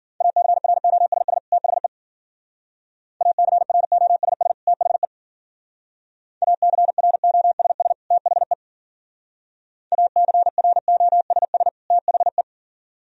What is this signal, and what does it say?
Morse code, heard as a keyed tone, about 49 wpm, 700 Hz. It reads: ACROSS THE ACROSS THE ACROSS THE ACROSS THE
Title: Speed-Racing - Sets of 2 Words 40wpm